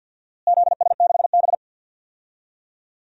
Morse code, heard as a keyed tone, about 50 wpm, 700 Hz. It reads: ZS6B
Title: DXCC-100 Call Signs - 4 Characters - Repeated 3x - 50wpm